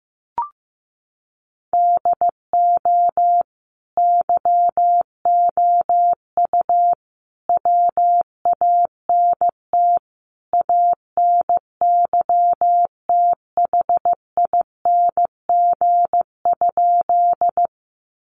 Morse code, heard as a keyed tone, about 15 wpm, 700 Hz. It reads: DO YOU WANT ANYTHING?
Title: Sentences from Top 1000 Words 15wpm